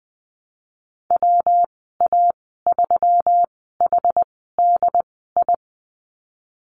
Morse code, heard as a keyed tone, about 19 wpm, spaced as for 14 wpm, 700 Hz. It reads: WA3HDI
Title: 179 Lesson 29g - Slash - Call Signs 20wpm